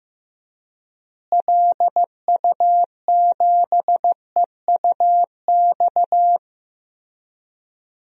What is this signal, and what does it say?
Morse code, heard as a keyed tone, about 15 wpm, 700 Hz. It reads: LU7EUX
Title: DXCC-100 Call Signs - 6 Characters - Repeated 3x - 15wpm